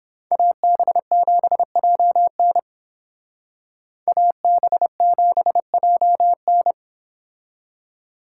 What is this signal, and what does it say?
Morse code, heard as a keyed tone, about 29 wpm, 700 Hz. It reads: AB7JN AB7JN